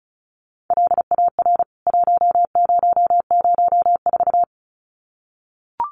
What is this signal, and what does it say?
Morse code, heard as a keyed tone, about 34 wpm, 700 Hz. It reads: LAR 1004
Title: CWT Exchange 35wpm